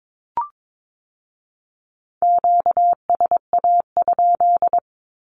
Morse code, heard as a keyed tone, about 22 wpm, 700 Hz. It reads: QSA?
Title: Q-Codes with Spoken Meaning - 22wpm